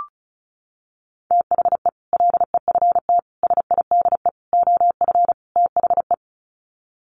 Morse code, heard as a keyed tone, about 34 wpm, 700 Hz. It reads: THE LEFT SIDE OF THE
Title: Sets of 5 Words Encore 35wpm